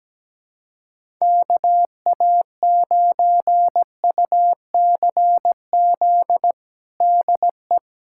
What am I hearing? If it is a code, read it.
Morse code, heard as a keyed tone, about 17 wpm, 700 Hz. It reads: KA9UCZ DE
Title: USA Callsign DE Callsign - 6 Characters - 17wpm